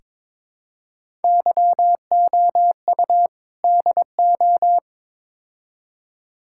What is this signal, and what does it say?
Morse code, heard as a keyed tone, about 22 wpm, 700 Hz. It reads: YOU DO
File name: practice / 22wpm / Build A Phrase - 5-Word Phrases - 22wpm